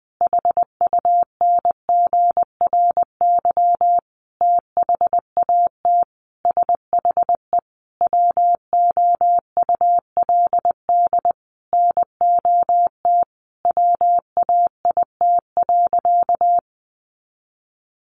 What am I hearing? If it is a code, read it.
Morse code, heard as a keyed tone, about 20 wpm, 700 Hz. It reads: HUNGRY THAT SHE WOULD NOT WAIT.
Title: Sentences from Top 1400 Words 20wpm